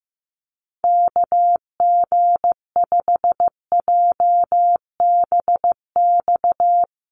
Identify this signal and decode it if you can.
Morse code, heard as a keyed tone, about 15 wpm, 700 Hz. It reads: KG5JBX
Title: USA Call Signs Repeated 3x 15wpm